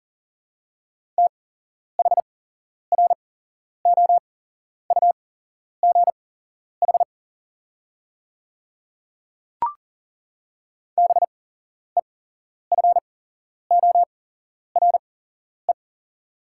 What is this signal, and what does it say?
Morse code, heard as a keyed tone, about 40 wpm, spaced as for 11 wpm, 690 Hz. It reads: THROUGH BEFORE
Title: 8x Character Spacing - Top 200 Words - 40wpm